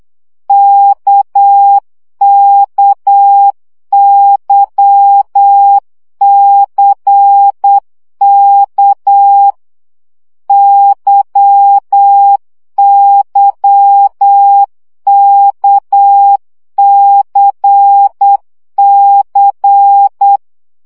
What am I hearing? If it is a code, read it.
Morse code, heard as a keyed tone, about 8 wpm, 800 Hz. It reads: KKYCK YYKCC
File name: CW gyakorlas k c y 7 baud